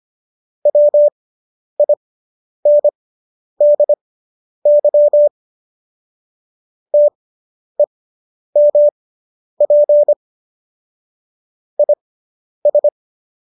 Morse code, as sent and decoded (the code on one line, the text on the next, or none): .-- .. -. -.. -.-- / - . -- .--. / .. ...
WINDY TEMP IS